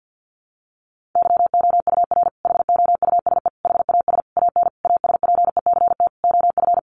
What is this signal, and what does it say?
Morse code, heard as a keyed tone, about 50 wpm, 700 Hz. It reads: YOUR HOUSE HAS AN ASPECT OF
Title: Sentences from Top 1300 Words 50wpm